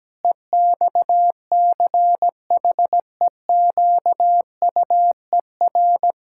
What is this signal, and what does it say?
Morse code, heard as a keyed tone, about 17 wpm, 700 Hz. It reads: EXCHEQUER
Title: Pangram Challenge v2 - 17wpm